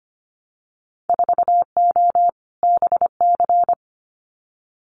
Morse code, heard as a keyed tone, about 25 wpm, 700 Hz. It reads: 4O BC
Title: Winter Field Day Exchanges Encore 25wpm